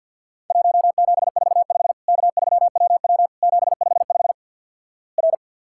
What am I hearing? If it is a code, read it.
Morse code, heard as a keyed tone, about 48 wpm, 700 Hz. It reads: 1745 K3WW 755 R
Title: Worked All Europe DX Contest - QTC Exchange - 50wpm